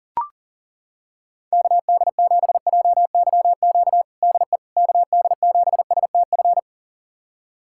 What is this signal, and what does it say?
Morse code, heard as a keyed tone, about 40 wpm, 700 Hz. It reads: KD7JYQ DE KD7STF
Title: USA Callsign DE Callsign 40wpm